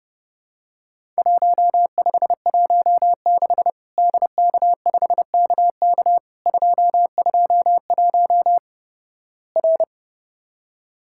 Morse code, heard as a keyed tone, about 30 wpm, 700 Hz. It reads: 1516 DK5KK 221 R
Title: Worked All Europe DX Contest - QTC Exchange - 30wpm